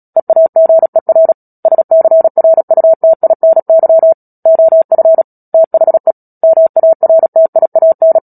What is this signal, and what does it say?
Morse code, heard as a keyed tone, about 37 wpm, 650 Hz. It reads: EAGER SCRUTINY OF THE MARTIAN
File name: Book Two / 10-Chapter 10 - Epilogue - 36 WPM